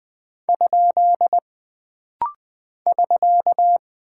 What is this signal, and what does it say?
Morse code, heard as a keyed tone, about 20 wpm, 700 Hz. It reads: ? <SK>
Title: Single Punctuation-Prosign - Rapid-Fire - 20wpm